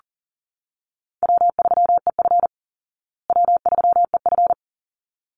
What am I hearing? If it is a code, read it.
Morse code, heard as a keyed tone, about 40 wpm, 700 Hz. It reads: W3EF W3EF
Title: Easy Call Signs Repeated 2x - 40wpm